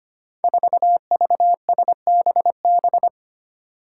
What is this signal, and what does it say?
Morse code, heard as a keyed tone, about 25 wpm, 700 Hz. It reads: 4VSBB
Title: Five Challenging Letter-Number Combinations 25wpm